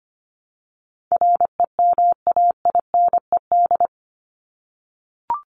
Morse code, heard as a keyed tone, about 25 wpm, 700 Hz. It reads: REMAINED
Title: ED Suffix - Top 100 Words with Base Words 25wpm